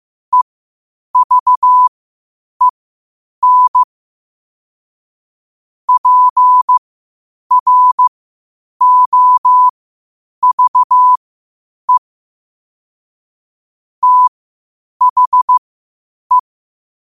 Morse code, heard as a keyed tone, about 15 wpm, spaced as for 8 wpm, 1000 Hz. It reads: EVEN PROVE THE
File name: qotdfiles / 8wpm